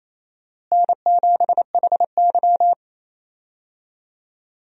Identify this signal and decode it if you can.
Morse code, heard as a keyed tone, about 28 wpm, 700 Hz. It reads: N7HY
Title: USA Call Signs - 4 Characters - Repeated 2x - 28wpm